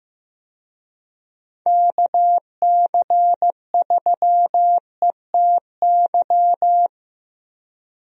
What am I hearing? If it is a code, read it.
Morse code, heard as a keyed tone, about 15 wpm, 700 Hz. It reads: KC3ETY
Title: USA Call Signs - 6 Characters - Repeated 2x - 15wpm